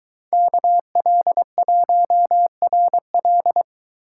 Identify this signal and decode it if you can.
Morse code, heard as a keyed tone, about 22 wpm, 700 Hz. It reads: KL1RL